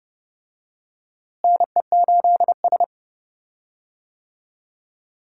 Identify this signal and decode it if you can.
Morse code, heard as a keyed tone, about 30 wpm, 700 Hz. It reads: NE8S
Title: USA Call Signs - 4 Characters - Repeated 3x - 30wpm